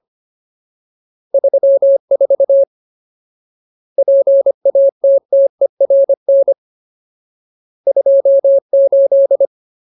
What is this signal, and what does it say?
Morse code, heard as a keyed tone, about 25 wpm, 550 Hz. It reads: 34 PATTERN 28
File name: Ultimate Word List / 25WPM Ultimate